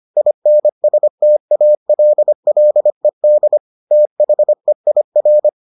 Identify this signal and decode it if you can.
Morse code, heard as a keyed tone, about 25 wpm, 600 Hz. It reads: INSTALLED THEIR